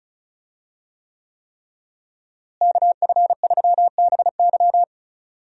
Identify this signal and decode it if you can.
Morse code, heard as a keyed tone, about 36 wpm, 700 Hz. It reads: KF3BY
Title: USA Call Signs - 5 Characters - Repeated 3x - 35wpm